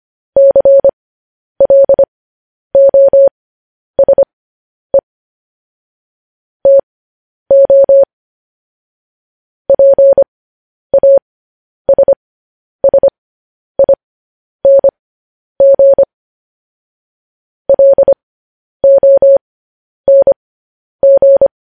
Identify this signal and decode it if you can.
Morse code, heard as a keyed tone, about 25 wpm, spaced as for 10 wpm, 550 Hz. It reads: CLOSE TO PASSING LONG
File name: podcast / News headlines-1x-25x10